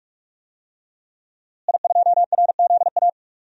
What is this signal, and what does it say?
Morse code, heard as a keyed tone, about 41 wpm, 700 Hz. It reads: I2RZA
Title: DXCC-100 Call Signs - 5 Characters - Repeated 2x - 45wpm